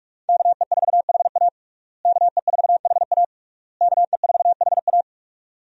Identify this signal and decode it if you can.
Morse code, heard as a keyed tone, about 46 wpm, 700 Hz. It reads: KE4HA KE4HA KE4HA
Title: USA Call Signs - 5 Characters - Repeated 3x - 45wpm